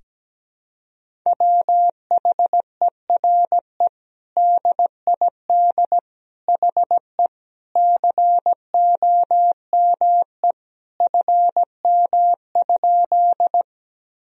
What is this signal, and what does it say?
Morse code, heard as a keyed tone, about 17 wpm, 700 Hz. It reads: WHERE DID HE COME FM?
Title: Sentences from Top 200 Words and Abbreviations 17wpm